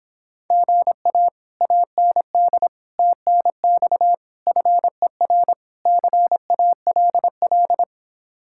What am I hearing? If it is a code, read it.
Morse code, heard as a keyed tone, about 26 wpm, 700 Hz. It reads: GA AND TNX FER CALL